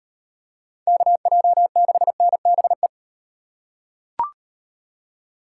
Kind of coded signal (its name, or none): Morse code